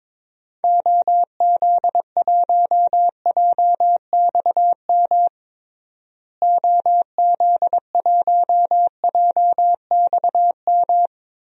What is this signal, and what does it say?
Morse code, heard as a keyed tone, about 22 wpm, 700 Hz. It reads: OZ1JXM OZ1JXM
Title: DXCC-100 Call Signs - 6 Characters - Repeated 2x - 22wpm